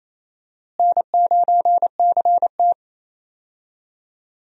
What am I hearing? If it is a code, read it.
Morse code, heard as a keyed tone, about 28 wpm, 700 Hz. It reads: N9CT
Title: USA Call Signs - 4 Characters - Repeated 3x - 28wpm